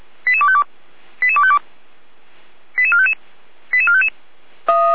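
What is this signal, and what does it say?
Five-tone selective call (ZVEI), heard as quick runs of five tones, tones of about 70 ms. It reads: 88252 88252 88455 88455
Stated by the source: ZVEI five-tone selective call 88191, received by SDR.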